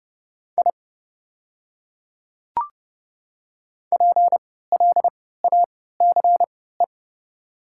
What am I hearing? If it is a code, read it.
Morse code, heard as a keyed tone, about 31 wpm, spaced as for 17 wpm, 700 Hz. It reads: I PLACE
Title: 3x Character Spacing - Top 200 Words - 30wpm